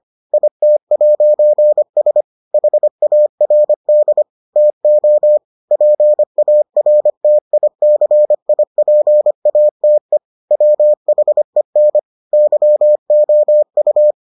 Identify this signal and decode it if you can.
Morse code, heard as a keyed tone, about 25 wpm, 600 Hz. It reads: IT'S HARD TO PARTICIPATE WHEN YOU